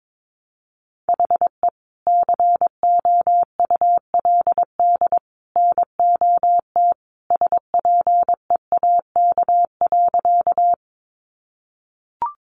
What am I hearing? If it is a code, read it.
Morse code, heard as a keyed tone, about 22 wpm, 700 Hz. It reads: HE COULD NOT SPEAK.
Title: Sentences from Top 1100 Words 22wpm